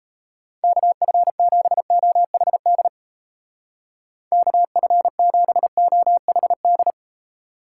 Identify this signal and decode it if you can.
Morse code, heard as a keyed tone, about 36 wpm, 700 Hz. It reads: KF7OHD KF7OHD